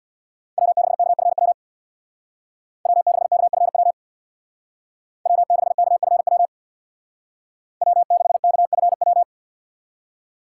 Morse code, heard as a keyed tone, about 62 wpm, 700 Hz. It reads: W6KFW W6KFW W6KFW W6KFW
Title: Speed Racing - USA 1x3 Call Signs 50wpm